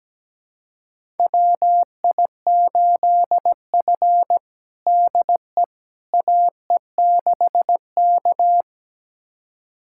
Morse code, heard as a keyed tone, about 17 wpm, 700 Hz. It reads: WI8F DE AE6K
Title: USA Callsign DE Callsign - 4 Characters - 17wpm